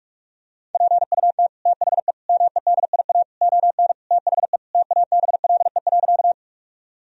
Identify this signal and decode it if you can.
Morse code, heard as a keyed tone, about 46 wpm, 700 Hz. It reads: PUT THE MEDIA ON THE TABLE.
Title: Sentences from Top 1000 Words - Review Part 4 - 45wpm